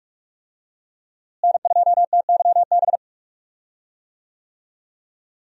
Morse code, heard as a keyed tone, about 46 wpm, 700 Hz. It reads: N2TYB
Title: USA Call Signs - 5 Characters - Repeated 3x - 45wpm